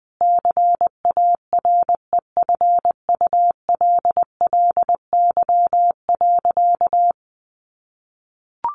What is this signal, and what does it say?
Morse code, heard as a keyed tone, about 20 wpm, 700 Hz. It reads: CAREFULLY.
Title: Sentences from Top 1300 Words 20wpm